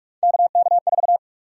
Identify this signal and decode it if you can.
Morse code, heard as a keyed tone, about 46 wpm, 700 Hz. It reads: KK4
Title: USA Call Sign Prefixes - 45wpm